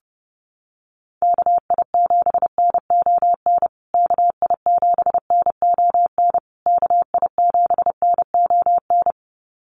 Morse code, heard as a keyed tone, about 30 wpm, 700 Hz. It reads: KI7NON KI7NON KI7NON
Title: USA Call Signs Repeated 3x 30wpm